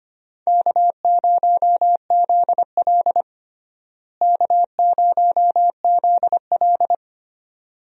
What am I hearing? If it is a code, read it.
Morse code, heard as a keyed tone, about 25 wpm, 700 Hz. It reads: K0ZL K0ZL
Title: USA Call Signs - 4 Characters - Repeated 2x - 25wpm